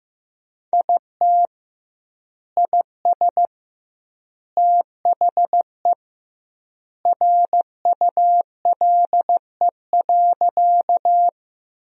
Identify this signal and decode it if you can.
Morse code, heard as a keyed tone, about 15 wpm, 700 Hz. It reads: IT IS THE RULE.
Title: 2x Word-Spacing - Sentences from Top 400 Words - 15wpm